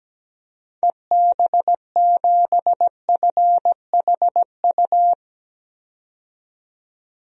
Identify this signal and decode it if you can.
Morse code, heard as a keyed tone, about 17 wpm, 700 Hz. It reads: EB7FHU
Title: DXCC-100 Call Signs - 6 Characters - 17wpm